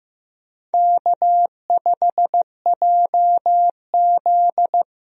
Morse code, heard as a keyed tone, about 15 wpm, 700 Hz. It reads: K5JZ
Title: USA Call Signs - 4 Characters - Repeated 2x - 15wpm